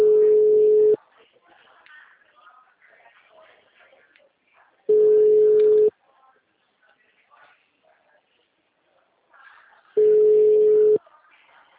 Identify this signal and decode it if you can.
Ringback tone (Europe, 425 Hz), with speech.